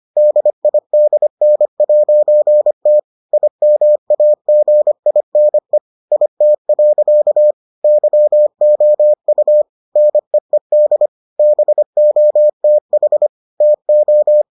Morse code, heard as a keyed tone, about 25 wpm, 600 Hz. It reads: DIDN'T IMAGINE IT. YOU NEED BOTH TO